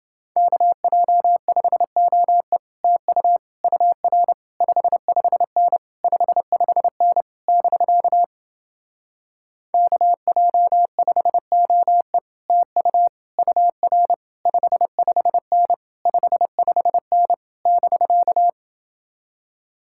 Morse code, heard as a keyed tone, about 29 wpm, 700 Hz. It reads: KJ5OE TU UR 55N 55N <BK> KJ5OE TU UR 55N 55N <BK>